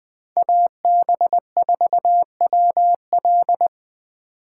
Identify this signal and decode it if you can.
Morse code, heard as a keyed tone, about 20 wpm, 700 Hz. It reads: AB4WL
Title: 2023 SOTA Chaser Call Signs - 20wpm